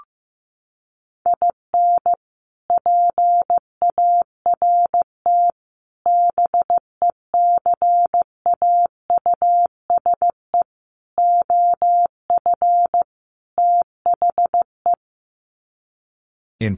Morse code, heard as a keyed tone, about 15 wpm, 700 Hz. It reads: IN PART BECAUSE OF THE
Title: Sets of 5 Words Encore2 15wpm